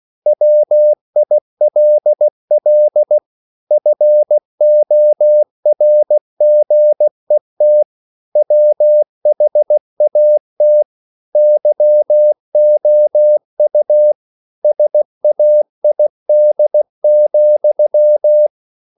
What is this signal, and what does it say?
Morse code, heard as a keyed tone, about 16 wpm, 600 Hz. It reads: WILL FORGET WHAT YOU SAID,
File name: CW / Practice / EnglishQuotations0216